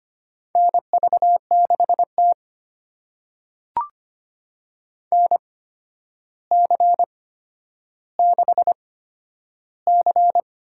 Morse code, heard as a keyed tone, about 25 wpm, spaced as for 11 wpm, 700 Hz. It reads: NV6T NC6C
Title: USA Call Signs - 4 Characters - 8x Character Spacing - 25wpm